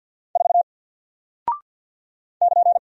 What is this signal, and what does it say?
Morse code, heard as a keyed tone, about 48 wpm, 700 Hz. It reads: 4 <KN>
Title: Single Character (Letter:Number:Punctuation:Prosign) Alt 50wpm